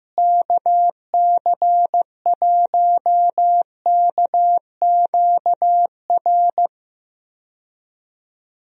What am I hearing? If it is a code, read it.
Morse code, heard as a keyed tone, about 15 wpm, 700 Hz. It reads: KC1KQR
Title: USA Call Signs - 6 Characters - Repeated 2x - 15wpm